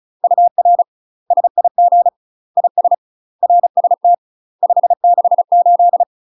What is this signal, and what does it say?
Morse code, heard as a keyed tone, about 36 wpm, 700 Hz. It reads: UR SIG IS RST 568
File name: UK MASTER QSOs TRAINING / 35WPM / UK MASTER QSOs 0014